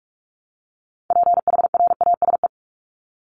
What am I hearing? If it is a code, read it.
Morse code, heard as a keyed tone, about 44 wpm, 700 Hz. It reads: PHRASE
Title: Top 1501-1750 Words 45wpm